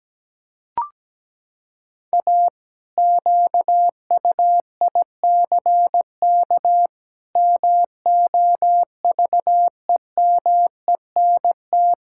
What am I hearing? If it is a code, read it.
Morse code, heard as a keyed tone, about 17 wpm, 700 Hz. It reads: A QUICK MOVEMENT
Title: Pangram Challenge v2 - 17wpm